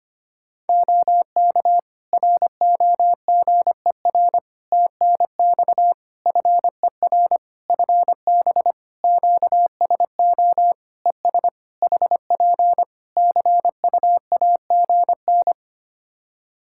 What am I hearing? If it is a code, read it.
Morse code, heard as a keyed tone, about 25 wpm, 700 Hz. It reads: OK ROGER TNX FER FB QSO ES HP CUAGN
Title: Practice QSO - Exchange 4 - Ending - 25wpm